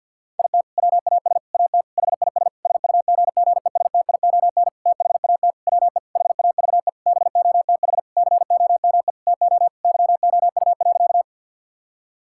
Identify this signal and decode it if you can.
Morse code, heard as a keyed tone, about 50 wpm, 700 Hz. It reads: IT WAS AT HIS SUGGESTION THAT WE HAVE BOTH COME TO YOU.
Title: Sentences from Top 1600 Words 50wpm